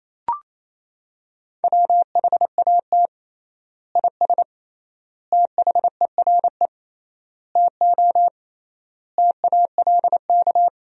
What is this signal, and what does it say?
Morse code, heard as a keyed tone, about 28 wpm, 700 Hz. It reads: WHAT IS THERE TO TALK
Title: 3x Word-Spacing - Sentences from Top 400 Words - 28wpm